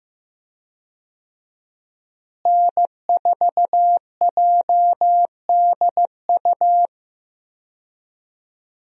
Morse code, heard as a keyed tone, about 15 wpm, 700 Hz. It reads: N4JDU
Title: USA Call Signs - 5 Characters - Repeated 2x - 15wpm